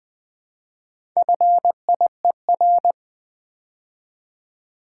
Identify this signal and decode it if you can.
Morse code, heard as a keyed tone, about 20 wpm, 700 Hz. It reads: FIER
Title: Build A Word - 2 Letters At A Time - 6-Letter Words - 20wpm